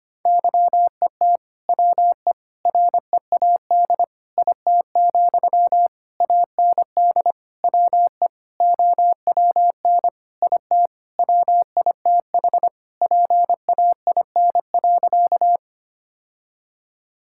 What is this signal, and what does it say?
Morse code, heard as a keyed tone, about 25 wpm, 700 Hz. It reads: YET WE READ IT, AND WE OWN IT WITH PAIN.
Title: Sentences from Top 1600 Words 25wpm